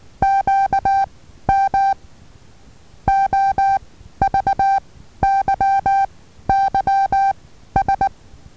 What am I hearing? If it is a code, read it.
Morse code, heard as a keyed tone, about 19 wpm, 780 Hz. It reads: Q M O V Y Y S
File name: Taviro HA6VA B